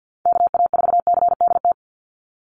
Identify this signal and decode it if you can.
Morse code, heard as a keyed tone, about 49 wpm, 700 Hz. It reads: KA4CDT
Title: USA Call Signs 50wpm